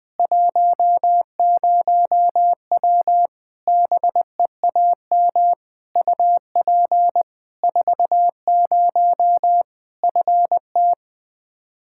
Morse code, heard as a keyed tone, about 20 wpm, 700 Hz. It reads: 10W BEAM UP 40 FT
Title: Practice QSO - Exchange 2 - Info - 20wpm